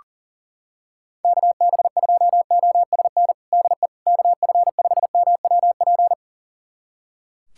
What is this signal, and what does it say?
Morse code, heard as a keyed tone, about 40 wpm, 700 Hz. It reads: KB2OSN DE KF5MWP
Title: USA Callsign DE Callsign 40wpm